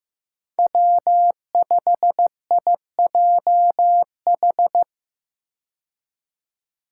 Morse code, heard as a keyed tone, about 15 wpm, 700 Hz. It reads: W5IJH
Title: K1USN SST QSO Practice - 15wpm